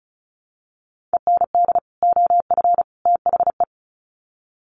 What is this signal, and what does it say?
Morse code, heard as a keyed tone, about 34 wpm, 700 Hz. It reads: END OF THE
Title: Sets of 3 Words 35wpm